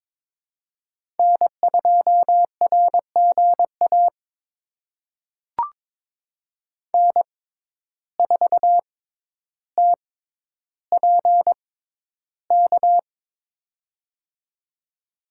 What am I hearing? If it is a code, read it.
Morse code, heard as a keyed tone, about 22 wpm, spaced as for 11 wpm, 700 Hz. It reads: N2RGA N4TPK
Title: USA Call Signs - 5 Characters - 6x Character Spacing - 22wpm